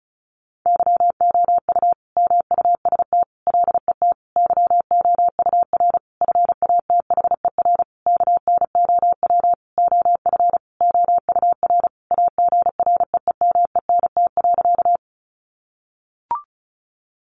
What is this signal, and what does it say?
Morse code, heard as a keyed tone, about 34 wpm, 700 Hz. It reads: YOU MUST LET YOUR FATHER KNOW OF OUR AGREEMENT.
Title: Sentences from Top 1500 Words 35wpm